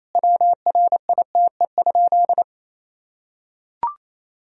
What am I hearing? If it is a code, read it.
Morse code, heard as a keyed tone, about 28 wpm, 700 Hz. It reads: WRITE?
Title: Sentences from Top 1000 Words - Review Part 1 - 28wpm